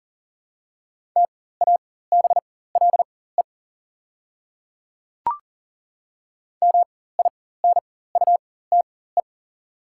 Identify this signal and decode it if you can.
Morse code, heard as a keyed tone, about 37 wpm, spaced as for 18 wpm, 700 Hz. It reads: TABLE MINUTE
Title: 4x Character Spacing - Top 500 Words - 40wpm